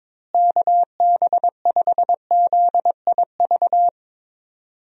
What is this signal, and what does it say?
Morse code, heard as a keyed tone, about 21 wpm, 700 Hz. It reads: KB5ZIV
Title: USA Call Signs - 6 Characters - Repeated 3x - 22wpm